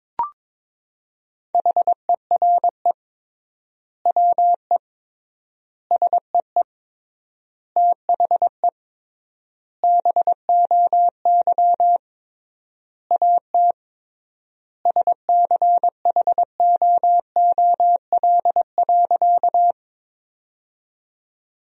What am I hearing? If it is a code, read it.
Morse code, heard as a keyed tone, about 22 wpm, 700 Hz. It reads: HERE WE SEE THE BOY AT SCHOOL.
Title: 3x Word-Spacing - Sentences from Top 300 Words - 22wpm